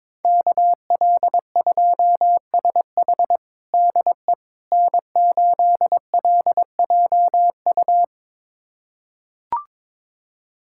Morse code, heard as a keyed tone, about 22 wpm, 700 Hz. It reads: KL2SH DE N8LJU
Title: USA Callsign DE Callsign 22wpm